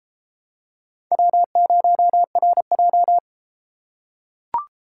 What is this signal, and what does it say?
Morse code, heard as a keyed tone, about 32 wpm, 700 Hz. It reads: W0RJ